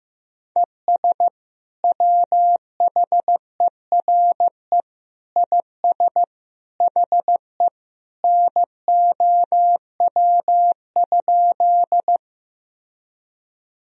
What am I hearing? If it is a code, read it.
Morse code, heard as a keyed tone, about 15 wpm, 700 Hz. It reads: ES WHERE IS HE NOW?
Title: Sentences from Top 200 Words and Abbreviations 15wpm